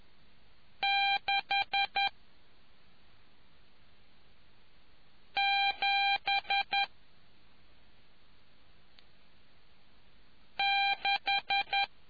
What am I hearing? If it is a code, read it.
Morse code, heard as a keyed tone, about 10 wpm, 800 Hz. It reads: 6 7 6